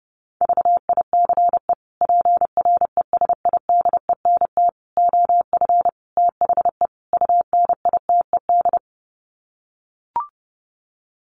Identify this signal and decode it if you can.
Morse code, heard as a keyed tone, about 30 wpm, 700 Hz. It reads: VICE PRESIDENT OF THE UNITED
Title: Sets of 5 Words Encore2 30wpm